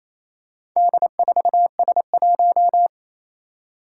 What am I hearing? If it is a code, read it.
Morse code, heard as a keyed tone, about 27 wpm, 700 Hz. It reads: D4S1
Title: Four Challenging Letter-Number Combinations 28wpm